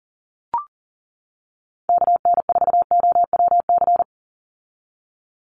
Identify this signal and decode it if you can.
Morse code, heard as a keyed tone, about 39 wpm, 690 Hz. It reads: KN4OWC
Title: USA Call Signs 40wpm